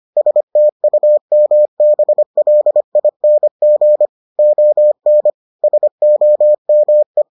Morse code, heard as a keyed tone, about 25 wpm, 600 Hz. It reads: STUMBLING ON SOME